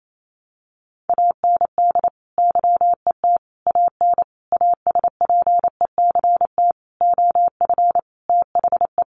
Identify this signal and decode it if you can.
Morse code, heard as a keyed tone, about 28 wpm, 700 Hz. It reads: AND YET AN ASPECT OF THE
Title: Sentences from Top 1300 Words 28wpm